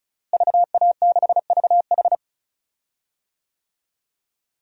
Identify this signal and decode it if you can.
Morse code, heard as a keyed tone, about 34 wpm, 700 Hz. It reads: VA6VH
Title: DXCC-100 Call Signs - 5 Characters - Repeated 3x - 35wpm